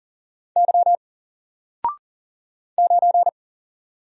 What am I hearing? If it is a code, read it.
Morse code, heard as a keyed tone, about 39 wpm, 700 Hz. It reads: Y 9
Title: Single Character (Letter:Number:Punctuation:Prosign) 40wpm